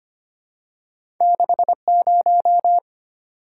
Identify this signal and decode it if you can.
Morse code, heard as a keyed tone, about 25 wpm, 700 Hz. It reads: 60
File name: Two Letter-Number Combinations - 25wpm